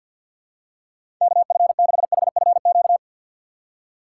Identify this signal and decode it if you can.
Morse code, heard as a keyed tone, about 53 wpm, 700 Hz. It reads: KF6HFQ
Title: USA Call Signs - 6 Characters - 8x Character Spacing - 50wpm